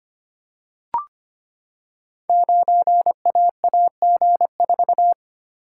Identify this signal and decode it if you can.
Morse code, heard as a keyed tone, about 25 wpm, 700 Hz. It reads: 9AAG4
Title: Code Groups - 5 Letter:Numbers - 25wpm